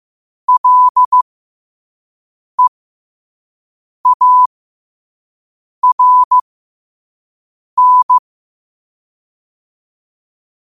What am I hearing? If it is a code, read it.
Morse code, heard as a keyed tone, about 15 wpm, spaced as for 5 wpm, 1000 Hz. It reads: LEARN
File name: qotdfiles / 5wpm